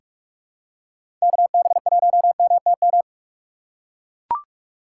Morse code, heard as a keyed tone, about 42 wpm, 700 Hz. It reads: KB1MTM
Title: USA Call Signs - 6 Characters - 6x Character Spacing - 45wpm